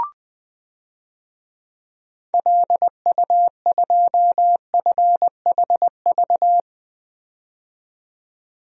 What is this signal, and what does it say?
Morse code, heard as a keyed tone, about 20 wpm, 700 Hz. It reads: LU2FHV
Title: DXCC-100 Call Signs - 6 Characters - 20wpm